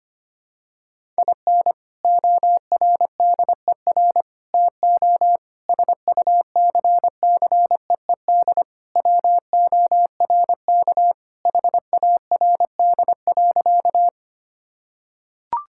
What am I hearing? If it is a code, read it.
Morse code, heard as a keyed tone, about 25 wpm, 700 Hz. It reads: IN ORDER TO SUCCEED WORK HARD.
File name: practice / 25wpm / Sentences with 3-Word Phrases - 25wpm